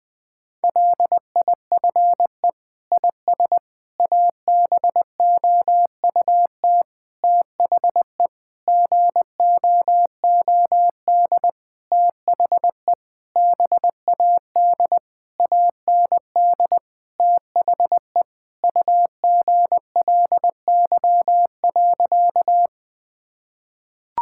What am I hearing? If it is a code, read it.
Morse code, heard as a keyed tone, about 20 wpm, 700 Hz. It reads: LIFE IS ABOUT THE GOOD THE BAD AND THE UGLY.
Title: Sentences with 6-Word Phrases - 20wpm